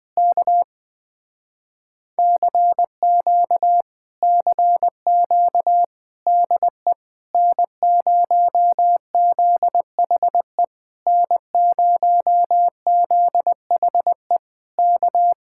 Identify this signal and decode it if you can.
Morse code, heard as a keyed tone, about 20 wpm, 700 Hz. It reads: K CQ CQ DE N0ZHE N0ZHE K
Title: Speed-Racing Common QSO Phrases Encore - 20wpm